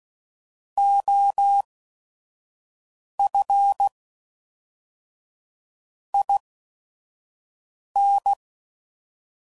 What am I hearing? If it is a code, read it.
Morse code, heard as a keyed tone, about 16 wpm, spaced as for 5 wpm, 800 Hz. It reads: OF IN
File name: CW Code Groups @5WPM / Technical Subjects / XL